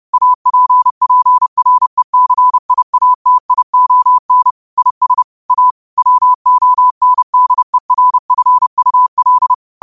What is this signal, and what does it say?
Morse code, heard as a keyed tone, about 30 wpm, 1000 Hz. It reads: APPRECIATION IS A WONDERFUL
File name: qotdfiles / 30wpm